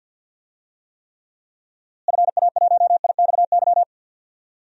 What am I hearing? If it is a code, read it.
Morse code, heard as a keyed tone, about 47 wpm, 700 Hz. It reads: FA1IXY
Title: DXCC-100 Call Signs - 6 Characters - Repeated 2x - 50wpm